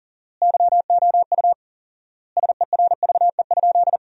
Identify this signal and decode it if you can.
Morse code, heard as a keyed tone, about 40 wpm, 700 Hz. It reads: YOU SERVE?
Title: 4x Word-Spacing - Sentences from Top 400 Words - 40wpm